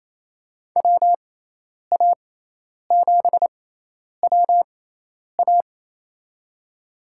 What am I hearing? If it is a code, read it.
Morse code, heard as a keyed tone, about 28 wpm, spaced as for 10 wpm, 700 Hz. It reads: WA7WA
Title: USA Call Signs - 5 Characters - 6x Character Spacing - 28wpm